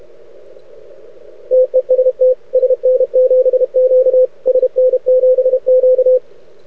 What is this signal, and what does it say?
Morse code, heard as a keyed tone, about 30 wpm, 500 Hz. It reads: TEST SN7Q SN7Q
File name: rec contest / arrl cw 2018 / test 32